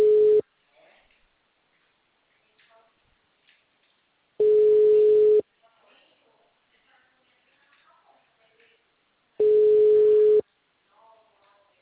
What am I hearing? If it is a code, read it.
Ringback tone (Europe, 425 Hz), with speech.
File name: out- -2014- - - .39403